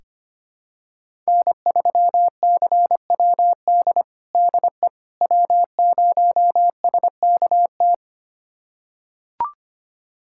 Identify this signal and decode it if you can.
Morse code, heard as a keyed tone, about 25 wpm, 700 Hz. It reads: N3CWD DE W0SKT
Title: USA Callsign DE Callsign - 5 Characters - 25wpm